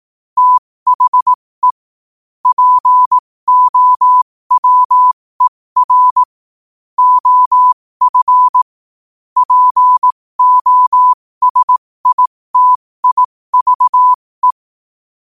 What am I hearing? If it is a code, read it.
Morse code, heard as a keyed tone, about 18 wpm, spaced as for 15 wpm, 1000 Hz. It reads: THE POWER OF POSITIVE
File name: qotdfiles / 15wpm